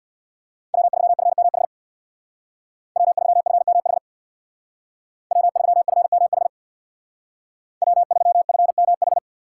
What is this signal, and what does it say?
Morse code, heard as a keyed tone, about 61 wpm, 700 Hz. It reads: W3FMH W3FMH W3FMH W3FMH
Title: Speed Racing - USA 1x3 Call Signs 50wpm